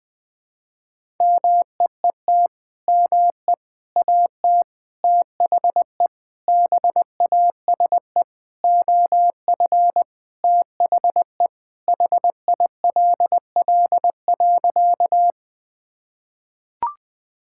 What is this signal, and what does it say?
Morse code, heard as a keyed tone, about 20 wpm, 700 Hz. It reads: MEET ME AT THE BASE OF THE HILL.
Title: Sentences with 5-Word Phrases - 20wpm